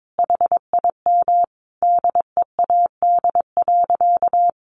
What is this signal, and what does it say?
Morse code, heard as a keyed tone, about 22 wpm, 700 Hz. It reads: HIM DEAD.
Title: Sentences from Top 1500 Words 22wpm